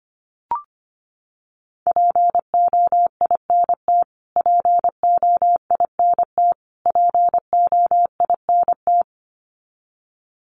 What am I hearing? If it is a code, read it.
Morse code, heard as a keyed tone, about 25 wpm, 700 Hz. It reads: POINT POINT POINT
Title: Top 300 Words Repeated 25wpm